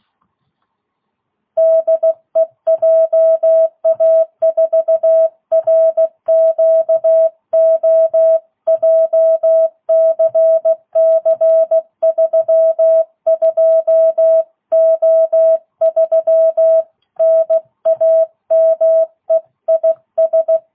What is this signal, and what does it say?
Morse code, heard as a keyed tone, about 16 wpm, 660 Hz. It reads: DE JA4RQO JCC32O3 NAME IS
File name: jh8cbh / 2022cw